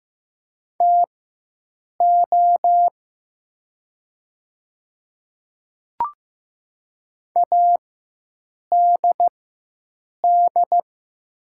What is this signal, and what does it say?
Morse code, heard as a keyed tone, about 15 wpm, spaced as for 7 wpm, 700 Hz. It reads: TO ADD
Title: 4x Character Spacing - Top 200 Words - 15wpm